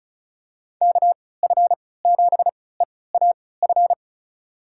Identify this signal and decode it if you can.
Morse code, heard as a keyed tone, about 34 wpm, spaced as for 20 wpm, 700 Hz. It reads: KF7EAF
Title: USA Call Signs - 3x Character Spacing - 35wpm